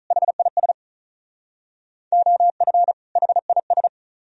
Morse code, heard as a keyed tone, about 43 wpm, 700 Hz. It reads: HIS OF HIS
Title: Speed-Racing - Sets of 2 Words 35wpm